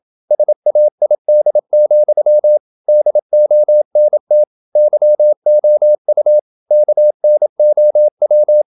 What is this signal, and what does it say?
Morse code, heard as a keyed tone, about 27 wpm, 600 Hz. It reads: SAID, DONT YOU KNOW
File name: Never 0027wpm